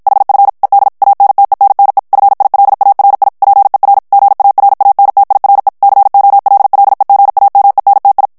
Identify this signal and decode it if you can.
Morse code, heard as a keyed tone, about 50 wpm, 800 Hz. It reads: HF ED ANTENNE FILARI PER GARANTIRE COLLEGAMENTI
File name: morse mp3 / ITALIEN texte clair 1 vitesse 50 mots minute